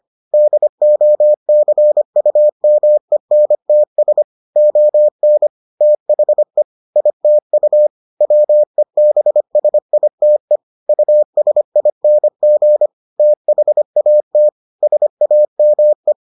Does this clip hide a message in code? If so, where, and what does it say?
Yes: Morse code from 0.3 to 16.1 s, heard as a keyed tone, about 25 wpm, 600 Hz. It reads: DOCUMENTS ON THE ITU WEBSITE USING THAT SAME